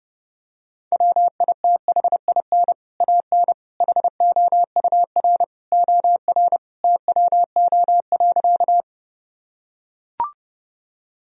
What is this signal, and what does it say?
Morse code, heard as a keyed tone, about 30 wpm, 700 Hz. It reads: WITHIN AN HOUR OR TWO.
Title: Sentences from Top 1000 Words - Review Part 4 - 30wpm